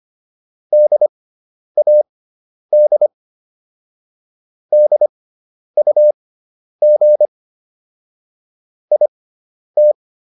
Morse code, heard as a keyed tone, about 25 wpm, spaced as for 10 wpm, 600 Hz. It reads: DAD DUG IT